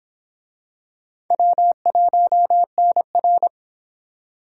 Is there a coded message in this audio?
Yes: Morse code from 1.3 to 3.5 s, heard as a keyed tone, about 22 wpm, 700 Hz. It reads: W1NR